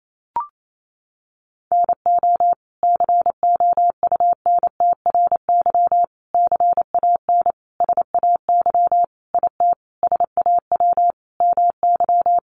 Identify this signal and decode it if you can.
Morse code, heard as a keyed tone, about 28 wpm, 700 Hz. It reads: NO COUNTRY CAN SAY IT SAW MY
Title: Sentences from Top 800 Words 28wpm